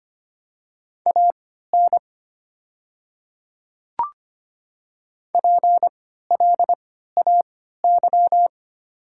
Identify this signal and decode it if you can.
Morse code, heard as a keyed tone, about 25 wpm, spaced as for 14 wpm, 700 Hz. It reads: AN PLAY
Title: 3x Character Spacing - Top 500 Words - 25wpm